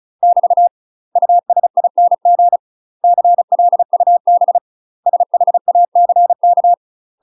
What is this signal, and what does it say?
Morse code, heard as a keyed tone, about 36 wpm, 700 Hz. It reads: <BT> USING CLUB SHACK